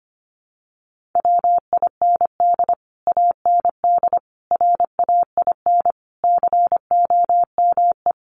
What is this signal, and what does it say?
Morse code, heard as a keyed tone, about 25 wpm, 700 Hz. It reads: WIND AND RAIN COME
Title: Sentences from Top 1400 Words 25wpm